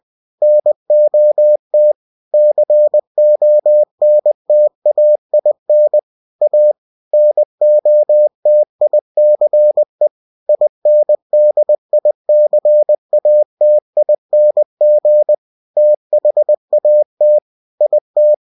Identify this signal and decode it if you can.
Morse code, heard as a keyed tone, about 20 wpm, 600 Hz. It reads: NOT CONTAIN A NOTICE INDICATING THAT IT
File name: cw-books / winnie-the-pooh / winnie-20wpm-page-0115